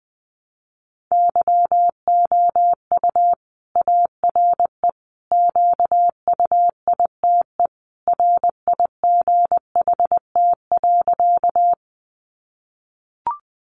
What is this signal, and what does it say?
Morse code, heard as a keyed tone, about 20 wpm, 700 Hz. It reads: YOU ARE QUITE RIGHT.
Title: Sentences from Top 700 Words 20wpm